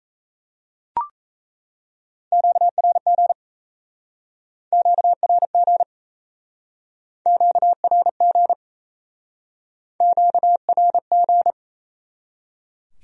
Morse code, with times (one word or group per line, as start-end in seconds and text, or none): 2.3-3.3 QRG
4.7-5.8 QRG
7.3-8.5 QRG
10.0-11.5 QRG